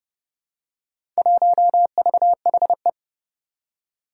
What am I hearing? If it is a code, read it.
Morse code, heard as a keyed tone, about 30 wpm, 700 Hz. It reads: 1VHE
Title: Four Challenging Letter-Number Combinations 30wpm